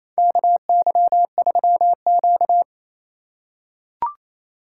Morse code, heard as a keyed tone, about 28 wpm, 700 Hz. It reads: KY3Q
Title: USA Call Signs - 4 Characters - 2x Character Spacing - 28wpm